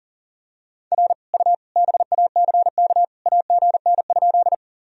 Morse code, heard as a keyed tone, about 40 wpm, 700 Hz. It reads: R U BACK AGN?
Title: Sentences from Top 200 Words and Abbreviations 40wpm